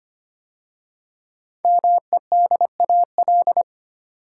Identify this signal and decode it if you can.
Morse code, heard as a keyed tone, about 25 wpm, 700 Hz. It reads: MEDAL
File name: 5 Letter Words - First 3 Letters Given - 25wpm